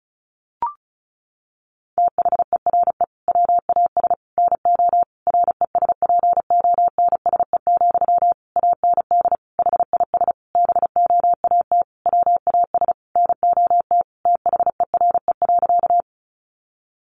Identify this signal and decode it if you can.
Morse code, heard as a keyed tone, about 34 wpm, 700 Hz. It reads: THERE WAS NO RESPONSE, AND HIS BOAT WAS NOT THERE.
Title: Sentences from Top 1400 Words 35wpm